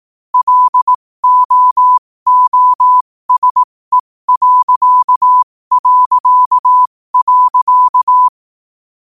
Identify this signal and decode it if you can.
Morse code, heard as a keyed tone, about 18 wpm, spaced as for 15 wpm, 1000 Hz. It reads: LOOSE...
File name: qotdfiles / 15wpm